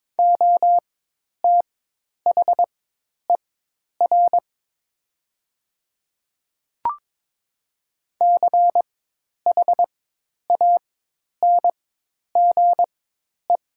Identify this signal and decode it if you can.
Morse code, heard as a keyed tone, about 22 wpm, spaced as for 10 wpm, 700 Hz. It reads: OTHER CHANGE
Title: 4x Character Spacing - Top 300 Words - 22wpm